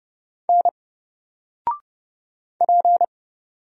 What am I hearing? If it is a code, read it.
Morse code, heard as a keyed tone, about 30 wpm, 700 Hz. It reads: N P
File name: Single-Letter-Number-30wpm